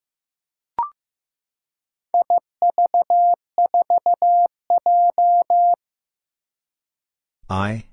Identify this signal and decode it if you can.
Morse code, heard as a keyed tone, about 15 wpm, 700 Hz. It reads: IV4J
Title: Four Challenging Letter-Number Combinations 15wpm